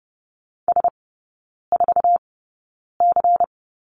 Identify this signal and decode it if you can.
Morse code, heard as a keyed tone, about 30 wpm, spaced as for 9 wpm, 700 Hz. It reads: S4C
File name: Lesson 18e - Letter m - Three Characters 30wpm